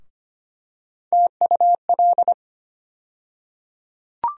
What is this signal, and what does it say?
Morse code, heard as a keyed tone, about 25 wpm, 700 Hz. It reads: TUL
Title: USA Call Sign Postfixes - 3 Letters 25wpm